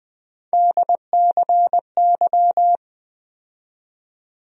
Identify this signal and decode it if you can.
Morse code, heard as a keyed tone, about 20 wpm, 700 Hz. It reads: DCY
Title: USA Call Sign Postfixes - 3 Letters 20wpm